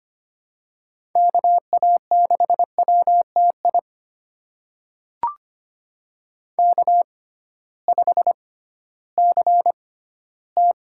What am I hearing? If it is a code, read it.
Morse code, heard as a keyed tone, about 25 wpm, spaced as for 15 wpm, 700 Hz. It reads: KA6WTI K5CT
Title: USA Call Signs - 6x Character Spacing - 25wpm